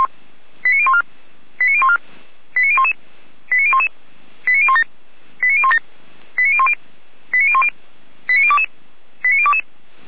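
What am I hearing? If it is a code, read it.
Five-tone selective call (ZVEI), heard as quick runs of five tones, tones of about 70 ms. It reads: 89014 89014 89011 89011 89017 89017 89010 89010 89022 89022
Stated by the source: ZVEI five-tone selective call 89060 received by SDR.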